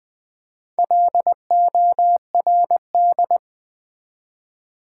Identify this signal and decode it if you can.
Morse code, heard as a keyed tone, about 20 wpm, 700 Hz. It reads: LORD
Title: Build A Word - 1 Letter At A Time - 4-Letter Words - 20wpm